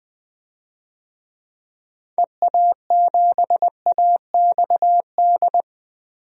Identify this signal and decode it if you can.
Morse code, heard as a keyed tone, about 21 wpm, 700 Hz. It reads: EA7AXD
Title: DXCC-100 Call Signs - 6 Characters - Repeated 3x - 20wpm